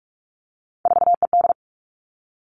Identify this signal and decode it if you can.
Morse code, heard as a keyed tone, about 46 wpm, 700 Hz. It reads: 4ED
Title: Three Challenging Letter-Number Combinations 45wpm